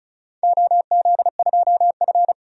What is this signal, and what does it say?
Morse code, heard as a keyed tone, about 36 wpm, 700 Hz. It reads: OZ2F
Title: DXCC-100 Call Signs - 4 Characters - 35wpm